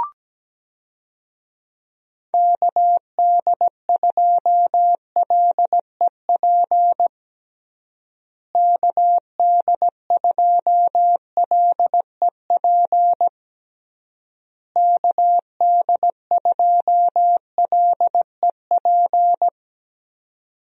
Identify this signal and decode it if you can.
Morse code, heard as a keyed tone, about 17 wpm, 700 Hz. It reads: KD2LEP KD2LEP KD2LEP
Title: USA Call Signs - 6 Characters - Repeated 3x - 17wpm